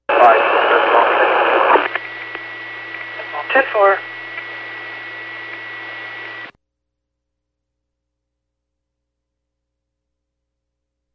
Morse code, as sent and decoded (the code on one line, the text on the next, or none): none